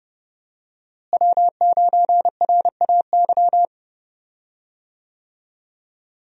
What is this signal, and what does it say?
Morse code, heard as a keyed tone, about 32 wpm, 700 Hz. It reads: W9RAY
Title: USA Call Signs - 5 Characters - Repeated 2x - 30wpm